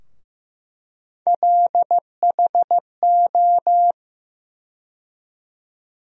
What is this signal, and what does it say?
Morse code, heard as a keyed tone, about 15 wpm, 700 Hz. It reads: LHO
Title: USA Call Sign Postfixes - 3 Letters 15wpm